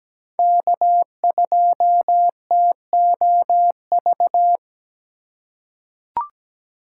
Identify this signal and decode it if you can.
Morse code, heard as a keyed tone, about 17 wpm, 700 Hz. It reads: K2TOV
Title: USA Call Signs - 5 Characters - 2x Character Spacing - 17wpm